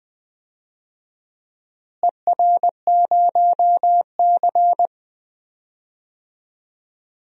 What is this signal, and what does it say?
Morse code, heard as a keyed tone, about 20 wpm, 700 Hz. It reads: ER0C
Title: DXCC-100 Call Signs - 4 Characters - Repeated 3x - 20wpm